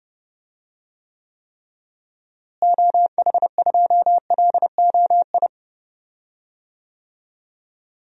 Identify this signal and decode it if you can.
Morse code, heard as a keyed tone, about 31 wpm, 700 Hz. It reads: OH2LOI
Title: DXCC-100 Call Signs - 6 Characters - Repeated 3x - 30wpm